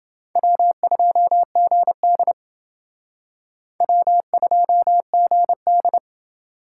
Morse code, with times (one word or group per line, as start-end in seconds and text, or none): 0.4-2.3 W2GD
3.8-6.0 W2GD